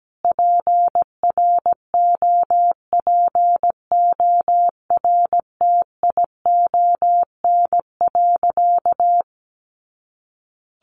Morse code, heard as a keyed tone, about 17 wpm, 700 Hz. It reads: PROPORTION.
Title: A Sound Of Thunder 17wpm